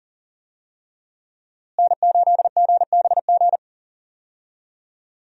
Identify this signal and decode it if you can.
Morse code, heard as a keyed tone, about 40 wpm, 700 Hz. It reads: N8GBG
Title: USA Call Signs - 5 Characters - Repeated 3x - 40wpm